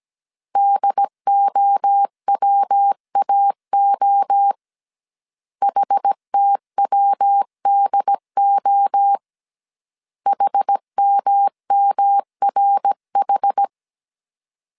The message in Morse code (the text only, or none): DOWAO HTWDO HMMRH